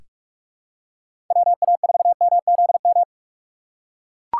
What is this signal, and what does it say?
Morse code, heard as a keyed tone, about 42 wpm, 700 Hz. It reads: WA4MZM
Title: USA Call Signs - 6 Characters - 4x Character Spacing - 45wpm